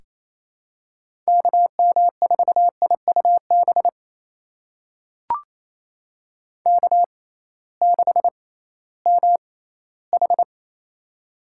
Morse code, heard as a keyed tone, about 28 wpm, spaced as for 16 wpm, 700 Hz. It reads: KM4IUB K6MH
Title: USA Call Signs - 6x Character Spacing - 28wpm